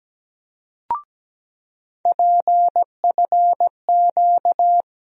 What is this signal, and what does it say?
Morse code, heard as a keyed tone, about 17 wpm, 700 Hz. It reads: PFQ
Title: USA Call Sign Postfixes - 3 Letters 17wpm